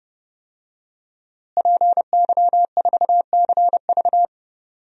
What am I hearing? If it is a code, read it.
Morse code, heard as a keyed tone, about 30 wpm, 700 Hz. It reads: PY4CV
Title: DXCC-100 Call Signs - 5 Characters - Repeated 2x - 30wpm